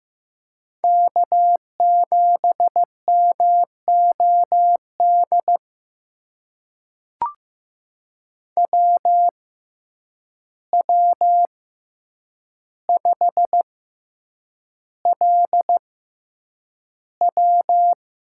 Morse code, heard as a keyed tone, about 15 wpm, spaced as for 8 wpm, 700 Hz. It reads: K7MOD WW5LW
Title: USA Call Signs - 6x Character Spacing - 15wpm